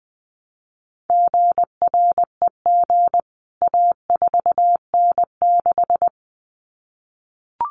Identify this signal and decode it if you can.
Morse code, heard as a keyed tone, about 20 wpm, 700 Hz. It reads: GREG A4N6
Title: CWT Exchange 20wpm